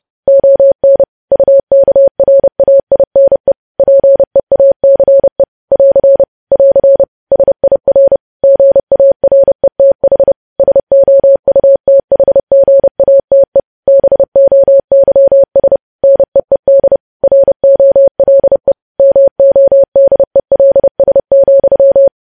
Morse code, heard as a keyed tone, about 30 wpm, 550 Hz. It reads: ON UKRAINE PEACE <AR> <AR> SIR GARETH SOUTHGATE BOYS NEED ROLE MODELS,
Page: news headlines in morse code at 30 wpm